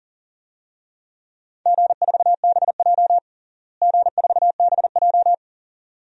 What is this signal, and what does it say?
Morse code, heard as a keyed tone, about 40 wpm, 700 Hz. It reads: G4BJ G4BJ
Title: DXCC-100 Call Signs - 4 Characters - Repeated 2x - 40wpm